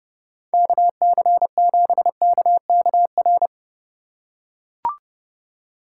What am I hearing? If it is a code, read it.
Morse code, heard as a keyed tone, about 29 wpm, 700 Hz. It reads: KC7KKR